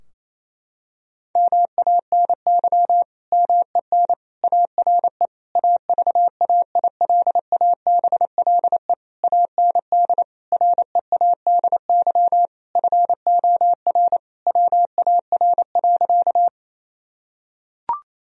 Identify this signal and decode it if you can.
Morse code, heard as a keyed tone, about 28 wpm, 700 Hz. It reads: MANY MEN ARE AVAILABLE AND READY FOR WAR.
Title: Sentences from Top 1000 Words - Review Part 3 - 28wpm